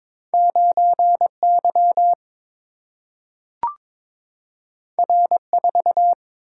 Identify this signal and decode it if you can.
Morse code, heard as a keyed tone, about 22 wpm, 700 Hz. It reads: 9 Y R 4
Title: Single-Letter Single-Number 22wpm